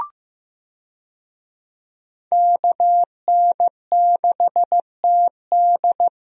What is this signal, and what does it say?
Morse code, heard as a keyed tone, about 15 wpm, 700 Hz. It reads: KN6TD
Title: USA Call Signs - 5 Characters - Repeated 2x - 15wpm